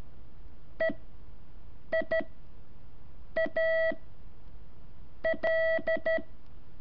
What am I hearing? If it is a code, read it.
Morse code, heard as a keyed tone, about 13 wpm, spaced as for 6 wpm, 650 Hz. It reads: EIA L
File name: curso / ejercicio 4